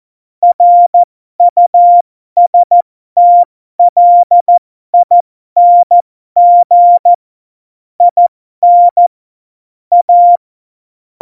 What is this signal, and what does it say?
Morse code, heard as a keyed tone, about 14 wpm, spaced as for 12 wpm, 700 Hz. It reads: RUSTLING IN A
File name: Morsekurs / Witze